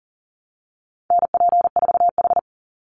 Morse code, heard as a keyed tone, about 40 wpm, 700 Hz. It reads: NP4H
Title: Call Signs Repeated 2x - 40wpm